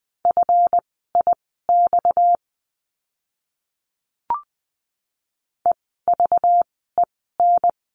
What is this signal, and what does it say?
Morse code, heard as a keyed tone, about 20 wpm, spaced as for 15 wpm, 700 Hz. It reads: FIX EVEN
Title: 230 Lesson 37f - Letter x - Words 20wpm